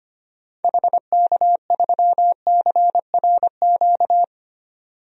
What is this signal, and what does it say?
Morse code, heard as a keyed tone, about 25 wpm, 700 Hz. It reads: HK3CRQ
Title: DXCC-100 Call Signs - 6 Characters - 25wpm